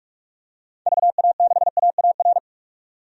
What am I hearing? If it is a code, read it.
Morse code, heard as a keyed tone, about 46 wpm, 700 Hz. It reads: VA6AAR
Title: DXCC-100 Call Signs - 6 Characters - 45wpm